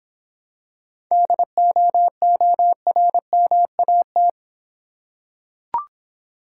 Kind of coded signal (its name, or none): Morse code